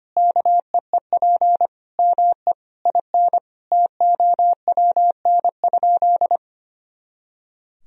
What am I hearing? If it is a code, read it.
Morse code, heard as a keyed tone, about 25 wpm, 700 Hz. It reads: KEEP ME IN TOWN?
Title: Sentences from Top 500 Words and Abbreviations 25wpm